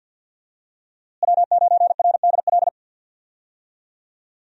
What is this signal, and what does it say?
Morse code, heard as a keyed tone, about 52 wpm, 700 Hz. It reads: W9RDL
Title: USA Call Signs - 5 Characters - Repeated 2x - 50wpm